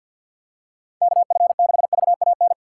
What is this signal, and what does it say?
Morse code, heard as a keyed tone, about 53 wpm, 700 Hz. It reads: KF6VAN
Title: USA Call Signs - 6 Characters - 6x Character Spacing - 50wpm